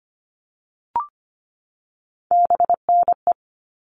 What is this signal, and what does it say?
Morse code, heard as a keyed tone, about 25 wpm, 700 Hz. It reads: BNE
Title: All Cut Numbers - 3-Digit 25wpm